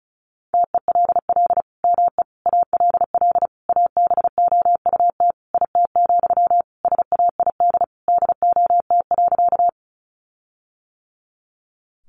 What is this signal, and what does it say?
Morse code, heard as a keyed tone, about 34 wpm, 700 Hz. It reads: TELL ME ALL ABOUT IT, SAID DOT.
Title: Sentences from Top 1500 Words 35wpm